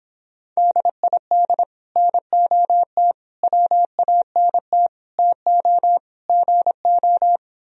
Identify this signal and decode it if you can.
Morse code, heard as a keyed tone, about 26 wpm, 700 Hz. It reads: DID NOT WANT TO GO